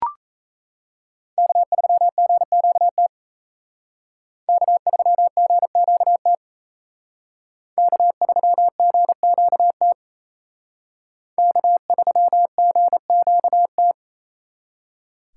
Morse code, heard as a keyed tone, about 35 wpm, 700 Hz. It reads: K3GQT K3GQT K3GQT K3GQT